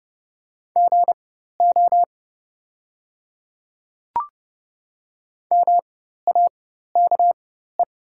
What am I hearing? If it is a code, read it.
Morse code, heard as a keyed tone, about 30 wpm, spaced as for 14 wpm, 690 Hz. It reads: GO MAKE
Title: 4x Character Spacing - Top 200 Words - 30wpm